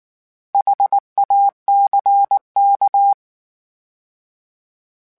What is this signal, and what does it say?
Morse code, heard as a keyed tone, about 19 wpm, 800 Hz. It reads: HACK
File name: H Prefixes / H mp3s